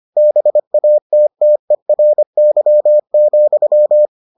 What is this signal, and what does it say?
Morse code, heard as a keyed tone, about 25 wpm, 600 Hz. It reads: BATTERY,